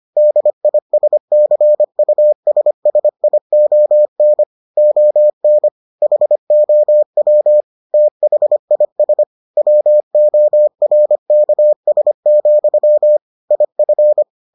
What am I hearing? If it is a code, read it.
Morse code, heard as a keyed tone, about 25 wpm, 600 Hz. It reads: DISCUSSION ON HOW THIS WORKS, IF